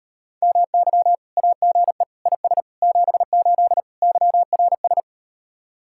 Morse code, heard as a keyed tone, about 38 wpm, 700 Hz. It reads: MY AGE IS 78 YRS